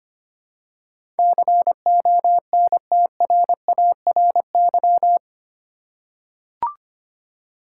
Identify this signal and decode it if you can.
Morse code, heard as a keyed tone, about 25 wpm, 700 Hz. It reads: CONTRARY
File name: Eight Letter Words - 25wpm